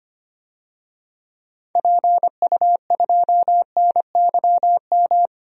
Morse code, heard as a keyed tone, about 25 wpm, 700 Hz. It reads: PU2NYM
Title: DXCC-100 Call Signs - 6 Characters - 25wpm